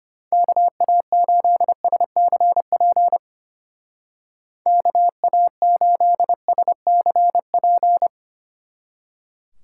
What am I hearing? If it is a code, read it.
Morse code, heard as a keyed tone, about 32 wpm, 700 Hz. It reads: KA8SCP KA8SCP